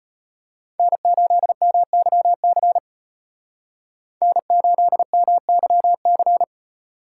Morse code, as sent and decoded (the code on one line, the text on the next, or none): -. ---.. -- -.-- -.-. / -. ---.. -- -.-- -.-.
N8MYC N8MYC